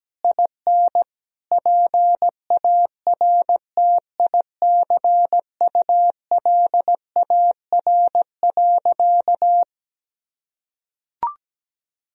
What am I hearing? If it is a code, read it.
Morse code, heard as a keyed tone, about 17 wpm, 700 Hz. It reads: IN PARTICULAR.
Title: Sentences from Top 1000 Words - Review Part 4 - 17wpm